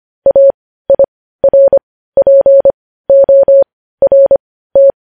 Morse code, heard as a keyed tone, about 25 wpm, spaced as for 15 wpm, 550 Hz. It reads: AIRPORT
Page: News Headlines in Morse Code at 15 WPM